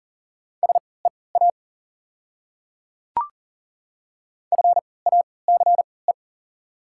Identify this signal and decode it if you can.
Morse code, heard as a keyed tone, about 40 wpm, spaced as for 23 wpm, 700 Hz. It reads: SEA FACE
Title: 3x Character Spacing - Top 300 Words - 40wpm